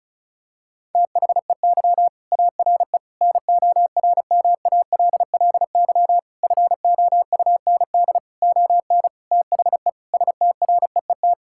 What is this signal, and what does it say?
Morse code, heard as a keyed tone, about 36 wpm, 700 Hz. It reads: THEY ARE NORMALLY FOUND ON THE STREET
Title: Sentences from Top 1000 Words - Review Part 6 - 35wpm